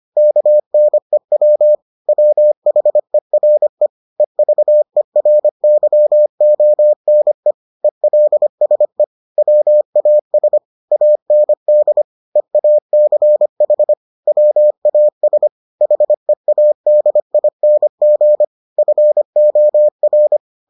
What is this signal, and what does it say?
Morse code, heard as a keyed tone, about 25 wpm, 600 Hz. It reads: KNEW WHERE EVERYONE ELSE WAS AND EACH WAS HEADING FOR